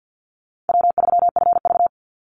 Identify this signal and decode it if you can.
Morse code, heard as a keyed tone, about 49 wpm, 700 Hz. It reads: W3FV
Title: Easy Call Signs Repeated 3x - 50wpm